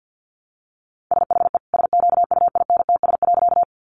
Morse code, heard as a keyed tone, about 50 wpm, 700 Hz. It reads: SHE SQUINTS.
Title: Additional Easy Phrases 50wpm